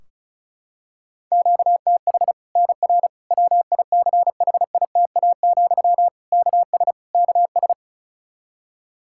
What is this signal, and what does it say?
Morse code, heard as a keyed tone, about 34 wpm, 700 Hz. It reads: QTH NR WICHITA, KS KS
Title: Practice QSO - Exchange 1 - Beginning - 35wpm